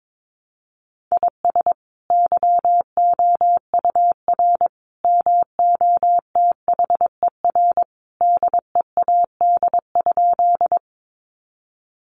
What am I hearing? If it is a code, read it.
Morse code, heard as a keyed tone, about 22 wpm, 700 Hz. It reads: IS YOUR MOTHER DEAD?
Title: Sentences from Top 1500 Words 22wpm